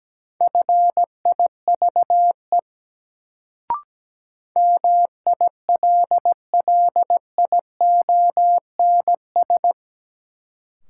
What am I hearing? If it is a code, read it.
Morse code, heard as a keyed tone, about 17 wpm, 700 Hz. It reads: FIVE MILLIONS
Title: Sets of 5 Words - New Word Drill - 17wpm